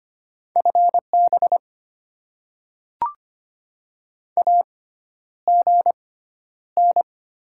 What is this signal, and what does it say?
Morse code, heard as a keyed tone, about 25 wpm, spaced as for 11 wpm, 690 Hz. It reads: FB AGN
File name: Top 20 QSO Elements - 6x Character Spacing - 25wpm